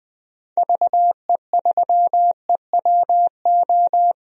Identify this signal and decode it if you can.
Morse code, heard as a keyed tone, about 20 wpm, 700 Hz. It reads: VE3EWO
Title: DXCC-100 Call Signs - 6 Characters - Repeated 3x - 20wpm